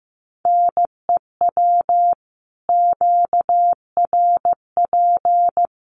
Morse code, heard as a keyed tone, about 15 wpm, 700 Hz. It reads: NEW QRP
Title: Common QSO Phrases 15wpm